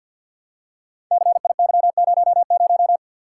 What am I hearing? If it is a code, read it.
Morse code, heard as a keyed tone, about 49 wpm, 700 Hz. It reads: CIY00
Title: Code Groups - 5 Letter:Numbers - 50wpm